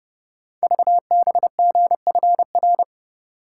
Morse code, heard as a keyed tone, about 30 wpm, 690 Hz. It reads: VBGFR
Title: Code Groups - 5 Letters - 30wpm